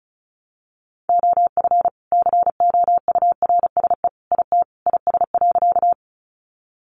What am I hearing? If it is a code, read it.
Morse code, heard as a keyed tone, about 34 wpm, 700 Hz. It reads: OF COURSE IT IS.
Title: Sentences from Top 500 Words 35wpm